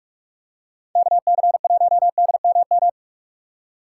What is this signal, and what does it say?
Morse code, heard as a keyed tone, about 46 wpm, 700 Hz. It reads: KC1DMM
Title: USA Call Signs - 6x Character Spacing - 45wpm